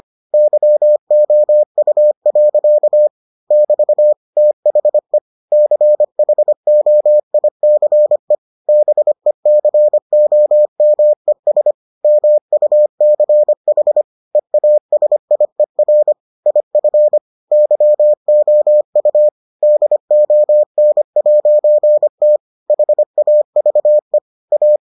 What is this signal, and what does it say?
Morse code, heard as a keyed tone, about 25 wpm, 600 Hz. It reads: YOU. <BT> THE CHOICE BECOMES MUCH EASIER IF YOU DON'T HAVE A